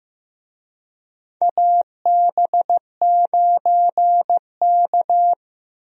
Morse code, heard as a keyed tone, about 15 wpm, 700 Hz. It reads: AB9K
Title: USA Call Signs - 4 Characters - Repeated 2x - 15wpm